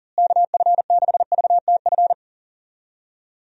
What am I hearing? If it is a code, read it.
Morse code, heard as a keyed tone, about 40 wpm, 700 Hz. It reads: KF6VTF
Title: USA Call Signs - 6 Characters - Repeated 3x - 40wpm